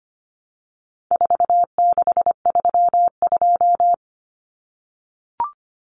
Morse code, heard as a keyed tone, about 25 wpm, 700 Hz. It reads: 4632
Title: Common Cut Numbers - 4-Digit 25wpm